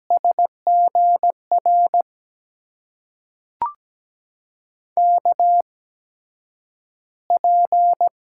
Morse code, heard as a keyed tone, about 17 wpm, spaced as for 9 wpm, 700 Hz. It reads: SGR KP
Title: USA Call Signs - 8x Character Spacing - 17wpm